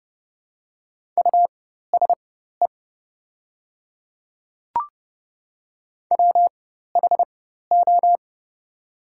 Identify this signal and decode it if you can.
Morse code, heard as a keyed tone, about 28 wpm, spaced as for 14 wpm, 700 Hz. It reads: USE WHO
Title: 4x Character Spacing - Top 100 Words - 30wpm